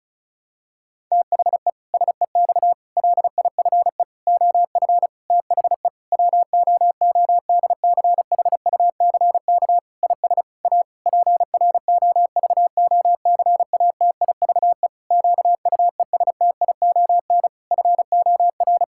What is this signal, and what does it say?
Morse code, heard as a keyed tone, about 36 wpm, 700 Hz. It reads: THE SEX LIFE OF THE WOODCHUCK IS A PROVOCATIVE QUESTION FOR
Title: Pangram Challenge v2 - 35wpm